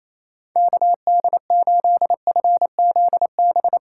Morse code, heard as a keyed tone, about 28 wpm, 700 Hz. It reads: KD8FZB
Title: USA Call Signs - 6 Characters - 4x Character Spacing - 28wpm